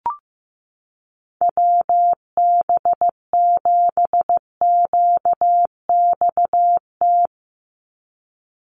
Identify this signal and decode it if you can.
Morse code, heard as a keyed tone, about 15 wpm, 700 Hz. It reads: WB7QXT
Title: USA Call Signs 15wpm